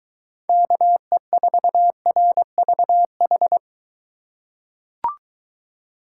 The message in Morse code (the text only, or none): KE4RVH